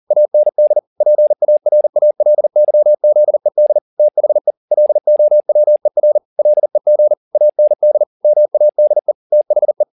Morse code, heard as a keyed tone, about 40 wpm, 600 Hz. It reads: AND PARALYZED THE LOWER LEG AND MADE THE
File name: the old man and the sea 2 40wpm